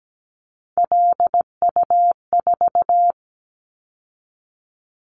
Morse code, heard as a keyed tone, about 17 wpm, 700 Hz. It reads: LU4
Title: Callsign Prefixes for Top 50 DXCC - 17wpm